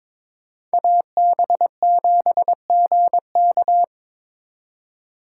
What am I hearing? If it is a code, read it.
Morse code, heard as a keyed tone, about 26 wpm, 700 Hz. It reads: AB7GK
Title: Speed Racing - USA 2x2 Call Signs 22wpm